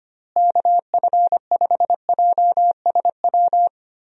Morse code, heard as a keyed tone, about 25 wpm, 700 Hz. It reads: KF5JSW
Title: USA Call Signs - 6 Characters - Repeated 3x - 25wpm